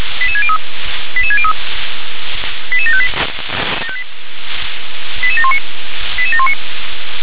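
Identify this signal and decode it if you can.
Five-tone selective call (ZVEI), heard as quick runs of five tones, tones of about 70 ms. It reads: 88603 88603 88758 88719 88719
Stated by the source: ZVEI five-tone selective call received on an SDR.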